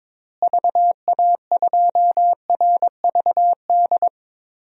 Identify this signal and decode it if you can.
Morse code, heard as a keyed tone, about 21 wpm, 700 Hz. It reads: VA2RVD
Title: DXCC-100 Call Signs - 6 Characters - Repeated 2x - 22wpm